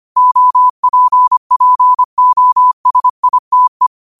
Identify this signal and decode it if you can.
Morse code, heard as a keyed tone, about 25 wpm, 1000 Hz. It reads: OPPOSITE
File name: qotdfiles / 25wpm